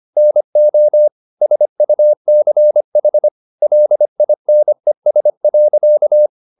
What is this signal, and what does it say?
Morse code, heard as a keyed tone, about 25 wpm, 600 Hz. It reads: NO SUCH LINES.